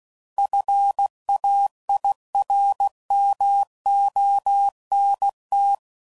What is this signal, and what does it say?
Morse code, heard as a keyed tone, about 16 wpm, 800 Hz. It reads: FAIRMONT
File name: CW Code Groups @15WPM